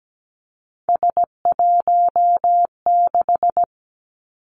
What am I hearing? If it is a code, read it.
Morse code, heard as a keyed tone, about 17 wpm, 700 Hz. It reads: S16
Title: Three Challenging Letter-Number Combinations 17wpm